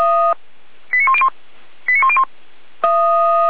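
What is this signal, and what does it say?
Five-tone selective call (ZVEI), heard as quick runs of five tones, tones of about 70 ms. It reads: 89101 89101
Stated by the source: ZVEI five-tone selective call received on SDR.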